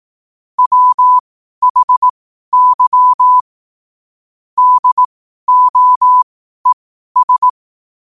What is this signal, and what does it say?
Morse code, heard as a keyed tone, about 18 wpm, spaced as for 12 wpm, 1000 Hz. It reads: WHY DOES
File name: qotdfiles / 12wpm